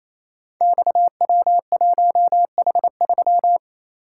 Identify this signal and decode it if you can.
Morse code, heard as a keyed tone, about 29 wpm, 700 Hz. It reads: XW1H3
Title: Code Groups - 5 Letter:Numbers - 28wpm